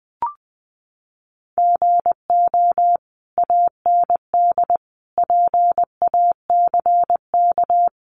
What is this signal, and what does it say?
Morse code, heard as a keyed tone, about 20 wpm, 700 Hz. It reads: GO AND PACK
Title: Sentences from Top 1300 Words 20wpm